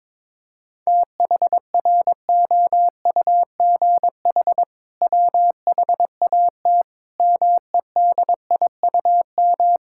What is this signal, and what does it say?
Morse code, heard as a keyed tone, about 22 wpm, 700 Hz. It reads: THROUGH WHAT MEDIUM
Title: Sentences from Top 1000 Words - Review Part 6 - 22wpm